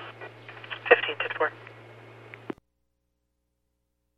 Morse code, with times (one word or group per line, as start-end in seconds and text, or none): none